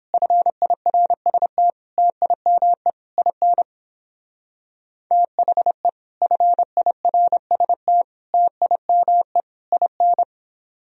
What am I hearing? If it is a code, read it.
Morse code, heard as a keyed tone, about 29 wpm, 700 Hz. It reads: FIRST TIME IN THE FIRST TIME IN